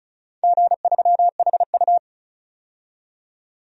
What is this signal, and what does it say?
Morse code, heard as a keyed tone, about 36 wpm, 700 Hz. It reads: G3HU
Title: DXCC-100 Call Signs - 4 Characters - 35wpm